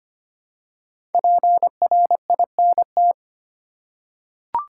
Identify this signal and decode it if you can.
Morse code, heard as a keyed tone, about 25 wpm, 700 Hz. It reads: PRINT
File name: 5 Letter Words - First 3 Letters Given - 25wpm